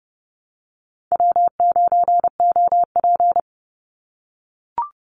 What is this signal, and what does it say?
Morse code, heard as a keyed tone, about 30 wpm, 700 Hz. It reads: W9OP
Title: Call Signs - 30wpm